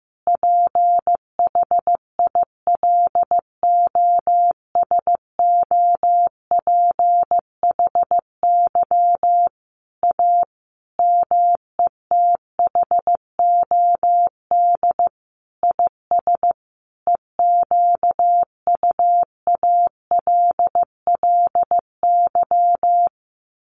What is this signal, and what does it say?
Morse code, heard as a keyed tone, about 15 wpm, 700 Hz. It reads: PHILOSOPHY A METHOD IS EQUALLY
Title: Sentences from Top 1400 Words 15wpm